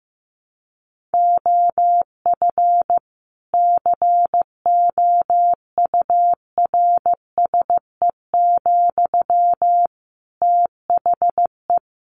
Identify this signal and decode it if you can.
Morse code, heard as a keyed tone, about 14 wpm, 700 Hz. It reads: OF COURSE, THE
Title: Sentences from Top 1500 Words 15wpm